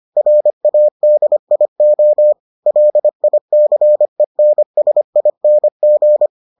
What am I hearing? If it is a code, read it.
Morse code, heard as a keyed tone, about 25 wpm, 600 Hz. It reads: RADIO LICENSING